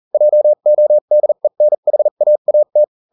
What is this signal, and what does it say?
Morse code, heard as a keyed tone, about 40 wpm, spaced as for 35 wpm, 600 Hz. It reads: JODENHAAT